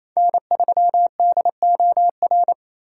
Morse code, heard as a keyed tone, about 28 wpm, 700 Hz. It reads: N3DOR
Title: USA Call Signs - 5 Characters - 2x Character Spacing - 28wpm